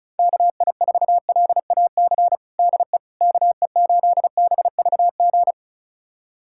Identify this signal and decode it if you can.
Morse code, heard as a keyed tone, about 36 wpm, 700 Hz. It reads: KI4LAC DE KE8BVG
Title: USA Callsign DE Callsign - 6 Characters - 35wpm